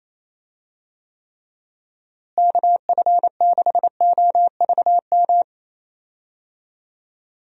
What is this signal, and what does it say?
Morse code, heard as a keyed tone, about 27 wpm, 700 Hz. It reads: KF6OVM
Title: USA Call Signs - 6 Characters - Repeated 2x - 28wpm